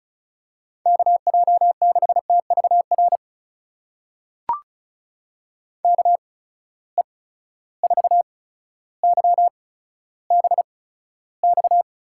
Morse code, heard as a keyed tone, about 36 wpm, spaced as for 15 wpm, 700 Hz. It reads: KJ6TVR KE4YBX
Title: USA Call Signs - 6 Characters - 8x Character Spacing - 35wpm